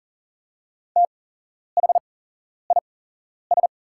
Morse code, heard as a keyed tone, about 40 wpm, spaced as for 11 wpm, 690 Hz. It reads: THIS
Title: 8x Character Spacing - Top 100 Words - 40wpm